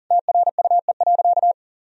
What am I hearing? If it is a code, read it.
Morse code, heard as a keyed tone, about 40 wpm, 700 Hz. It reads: TRUE.
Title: Sentences from Top 1000 Words - Review Part 3 - 40wpm